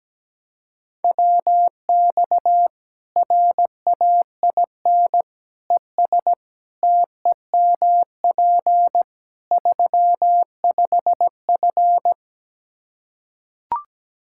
Morse code, heard as a keyed tone, about 17 wpm, 700 Hz. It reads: WX RAIN ES TEMP 35F
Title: Practice QSO - Exchange 2 - Info - 17wpm